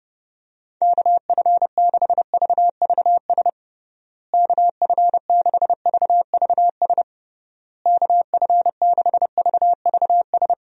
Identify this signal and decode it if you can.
Morse code, heard as a keyed tone, about 32 wpm, 700 Hz. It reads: KF6VVS KF6VVS KF6VVS
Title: USA Call Signs - 6 Characters - Repeated 3x - 30wpm